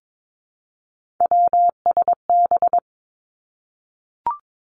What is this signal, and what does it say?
Morse code, heard as a keyed tone, about 22 wpm, 700 Hz. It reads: WSB
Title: Three Challenging Letter-Number Combinations 22wpm